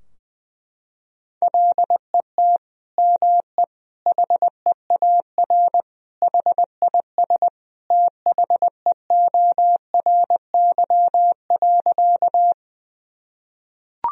Morse code, heard as a keyed tone, about 20 wpm, 700 Hz. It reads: LET ME HEAR HIS THEORY.
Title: Sentences from Top 1000 Words - Review Part 3 - 20wpm